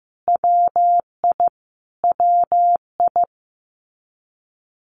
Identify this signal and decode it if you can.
Morse code, heard as a keyed tone, about 15 wpm, 700 Hz. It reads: WI WI
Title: US States Repeated 3x 15wpm